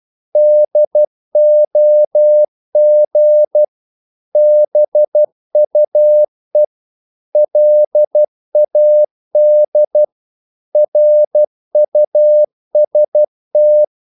Morse code, heard as a keyed tone, about 12 wpm, 600 Hz. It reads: DOG BUE LAD RUST
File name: Korte ord DK 12wpm